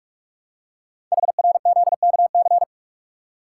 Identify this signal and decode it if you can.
Morse code, heard as a keyed tone, about 44 wpm, 690 Hz. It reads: HRZKC
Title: Code Groups - 5 Letters - 45wpm